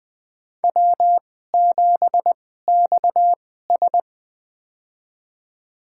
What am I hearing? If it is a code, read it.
Morse code, heard as a keyed tone, about 20 wpm, spaced as for 14 wpm, 700 Hz. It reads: W7XS
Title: USA Call Signs - 4 Characters - 2x Character Spacing - 20wpm